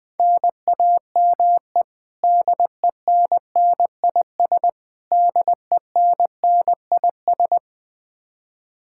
Morse code, heard as a keyed tone, about 20 wpm, 700 Hz. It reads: NAME DENNIS DENNIS
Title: Practice QSO - Exchange 1 - Beginning - 20wpm